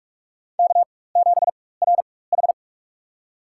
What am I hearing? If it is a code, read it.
Morse code, heard as a keyed tone, about 44 wpm, spaced as for 21 wpm, 700 Hz. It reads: K7RH
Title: USA Call Signs - 4 Characters - 4x Character Spacing - 45wpm